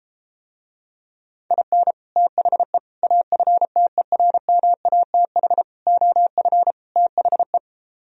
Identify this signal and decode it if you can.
Morse code, heard as a keyed tone, about 33 wpm, 700 Hz. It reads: IN THE AFTERMATH OF THE